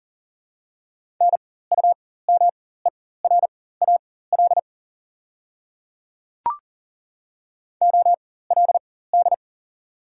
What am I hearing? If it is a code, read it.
Morse code, heard as a keyed tone, about 40 wpm, spaced as for 19 wpm, 690 Hz. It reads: NUMERAL OLD
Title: 4x Character Spacing - Top 500 Words - 40wpm